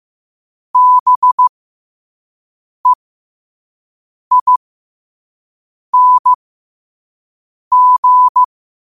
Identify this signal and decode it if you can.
Morse code, heard as a keyed tone, about 15 wpm, spaced as for 5 wpm, 1000 Hz. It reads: BEING
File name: qotdfiles / 5wpm